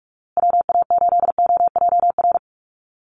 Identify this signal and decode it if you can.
Morse code, heard as a keyed tone, about 44 wpm, 700 Hz. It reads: WA8OJR
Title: Call Signs Repeated 2x - 45wpm